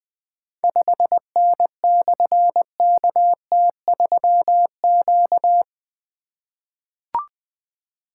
Morse code, heard as a keyed tone, about 20 wpm, 700 Hz. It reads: 5N/KT3Q
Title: Difficult Call Signs 20wpm